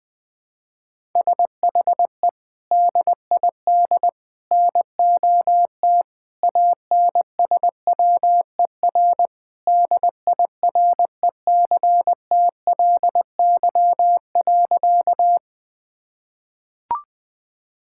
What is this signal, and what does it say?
Morse code, heard as a keyed tone, about 21 wpm, 700 Hz. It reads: SHE DID NOT ANSWER DIRECTLY.
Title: Sentences from Top 1000 Words - Review Part 5 - 20wpm